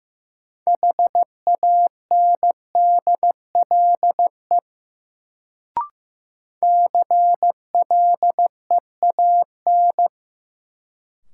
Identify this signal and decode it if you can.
Morse code, heard as a keyed tone, about 15 wpm, 700 Hz. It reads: HANDLE CLEAN
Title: Sentences from Top 1100 Words - New Word Drill Alt - 15wpm